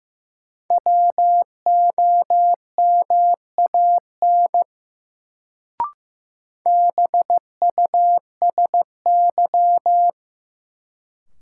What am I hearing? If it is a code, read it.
Morse code, heard as a keyed tone, about 15 wpm, 700 Hz. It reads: WOMAN BUSY
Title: Sentences from Top 1000 Words - Review Part 2 - New Word Drill - 15wpm